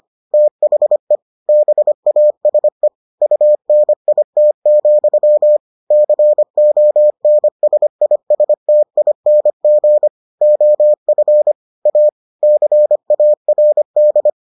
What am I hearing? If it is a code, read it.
Morse code, heard as a keyed tone, about 25 wpm, 600 Hz. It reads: THE BASE UNIT, CONSISTING OF A CARD